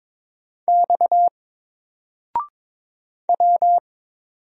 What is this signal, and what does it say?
Morse code, heard as a keyed tone, about 22 wpm, 700 Hz. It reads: X W
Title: Single Character (Letter:Number:Punctuation:Prosign) 22wpm